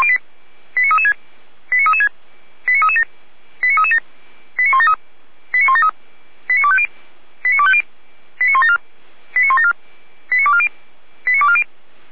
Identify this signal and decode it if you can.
Five-tone selective call (ZVEI), heard as quick runs of five tones, tones of about 70 ms. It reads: 89336 89336 89337 89337 89173 89173 89250 89250 89174 89174 89240 89240